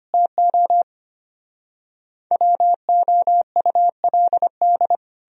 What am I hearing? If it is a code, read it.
Morse code, heard as a keyed tone, about 33 wpm, 700 Hz. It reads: TO WOULD